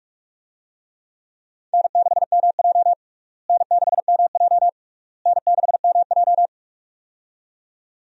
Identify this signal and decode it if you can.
Morse code, heard as a keyed tone, about 42 wpm, 700 Hz. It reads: N6MJ N6MJ N6MJ
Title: USA Call Signs - 4 Characters - Repeated 3x - 45wpm